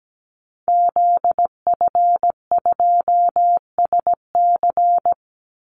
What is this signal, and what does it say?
Morse code, heard as a keyed tone, about 17 wpm, 700 Hz. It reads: ZF2SC
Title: Call Signs Repeated 3x - 17wpm